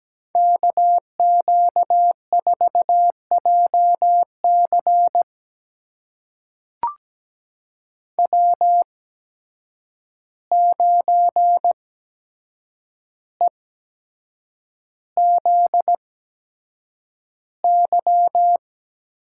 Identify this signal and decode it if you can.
Morse code, heard as a keyed tone, about 17 wpm, spaced as for 7 wpm, 700 Hz. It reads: KQ4JC W9EZY
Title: USA Call Signs - 5 Characters - 8x Character Spacing - 17wpm